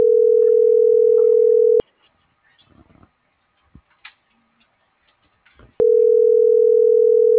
Ringback tone (North America), 440 + 480 Hz.